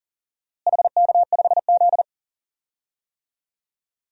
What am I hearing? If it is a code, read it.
Morse code, heard as a keyed tone, about 37 wpm, 700 Hz. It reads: HK5Z
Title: DXCC-100 Call Signs - 4 Characters - Repeated 2x - 40wpm